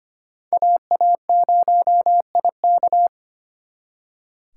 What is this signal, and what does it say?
Morse code, heard as a keyed tone, about 33 wpm, 700 Hz. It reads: AA0IK